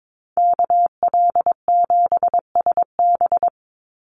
Morse code, heard as a keyed tone, about 22 wpm, 700 Hz. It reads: KL7SB
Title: Call Signs Repeated 2x - 22wpm